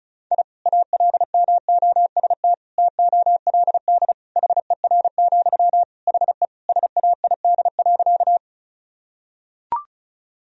Morse code, heard as a keyed tone, about 34 wpm, 700 Hz. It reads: I ALMOST TOLD HER, HE SAID.
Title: Sentences from Top 1000 Words - Review Part 4 - 35wpm